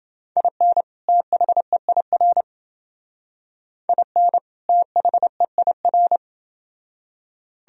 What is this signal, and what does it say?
Morse code, heard as a keyed tone, about 28 wpm, 700 Hz. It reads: IN THEIR IN THEIR